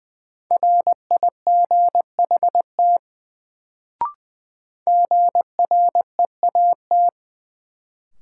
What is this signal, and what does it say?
Morse code, heard as a keyed tone, about 20 wpm, 700 Hz. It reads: RIGHT GREAT
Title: Sets of 2 Words - New Word Drill - 20wpm